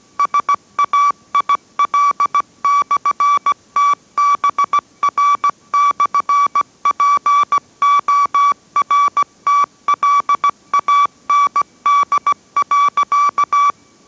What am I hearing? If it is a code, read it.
Morse code, heard as a keyed tone, about 17 wpm, 1150 Hz. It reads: SAIL/TBR/PORTLAND.